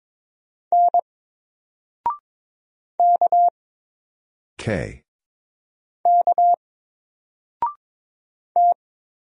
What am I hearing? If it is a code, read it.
Morse code, heard as a keyed tone, about 22 wpm, 700 Hz. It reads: N K K T
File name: Single-Letter-Number-22wpm